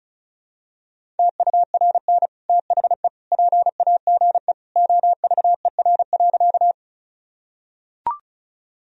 Morse code, heard as a keyed tone, about 36 wpm, 700 Hz. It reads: TURN THE PAGE OVER.
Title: Sentences from Top 1000 Words - Review Part 1 - 35wpm